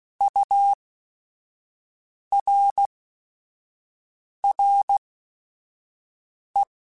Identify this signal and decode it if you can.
Morse code, heard as a keyed tone, about 16 wpm, spaced as for 5 wpm, 800 Hz. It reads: URRE
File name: Technical Subjects / Technical Subjects-mp3s / XC @5